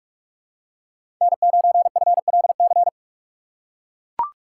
Morse code, heard as a keyed tone, about 42 wpm, 700 Hz. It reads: N9FLC
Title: USA Call Signs - 2x Character Spacing - 45wpm